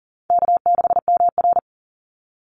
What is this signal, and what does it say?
Morse code, heard as a keyed tone, about 41 wpm, 700 Hz. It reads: K6MR
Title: Easy Call Signs - 40wpm